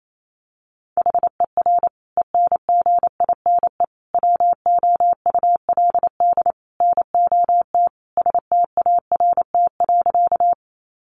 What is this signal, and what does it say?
Morse code, heard as a keyed tone, about 28 wpm, 700 Hz. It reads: HER ENGINE WOULD NOT START.
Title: Sentences from Top 1500 Words 28wpm